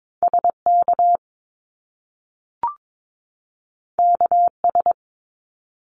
Morse code, heard as a keyed tone, about 22 wpm, 700 Hz. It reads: SK KS
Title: Top Two Letter Combinations in Words 22wpm